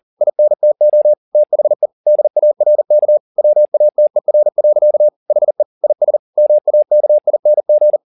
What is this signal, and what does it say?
Morse code, heard as a keyed tone, about 40 wpm, 600 Hz. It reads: INTO THE DARK WATER. HE IS MAKING
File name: the old man and the sea 2 40wpm